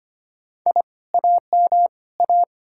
Morse code, heard as a keyed tone, about 33 wpm, 700 Hz. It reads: I AM A